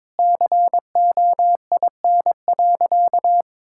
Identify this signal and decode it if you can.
Morse code, heard as a keyed tone, about 22 wpm, 700 Hz. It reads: COIN.
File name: Sentences with 7-Word Phrases - 22wpm